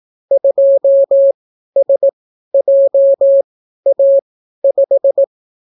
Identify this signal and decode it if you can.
Morse code, heard as a keyed tone, about 18 wpm, spaced as for 12 wpm, 550 Hz. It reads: 2SJA5